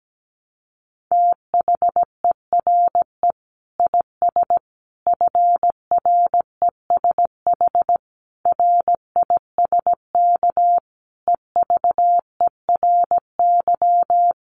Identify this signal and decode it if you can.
Morse code, heard as a keyed tone, about 17 wpm, 700 Hz. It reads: THERE IS FRESH RISK EVERY
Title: Sentences from Top 1200 Words 17wpm